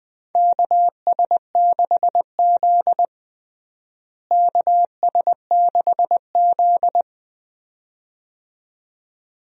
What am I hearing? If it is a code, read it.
Morse code, heard as a keyed tone, about 20 wpm, 700 Hz. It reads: KS6Z KS6Z
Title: USA Call Signs - 4 Characters - Repeated 2x - 20wpm